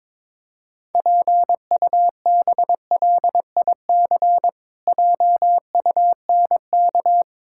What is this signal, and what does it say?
Morse code, heard as a keyed tone, about 22 wpm, 700 Hz. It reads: PUBLIC JUNK
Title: Pangram Challenge v2 - 22wpm